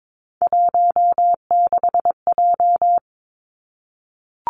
Morse code, heard as a keyed tone, about 22 wpm, 700 Hz. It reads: 16J
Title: Three Challenging Letter-Number Combinations 22wpm